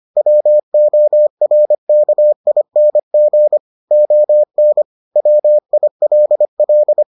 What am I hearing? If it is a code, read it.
Morse code, heard as a keyed tone, about 25 wpm, 600 Hz. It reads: WORKING ON WILL